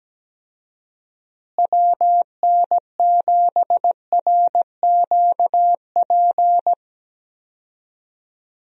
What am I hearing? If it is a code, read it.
Morse code, heard as a keyed tone, about 17 wpm, 700 Hz. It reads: WN7RQP
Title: USA Call Signs - 6 Characters - Repeated 2x - 17wpm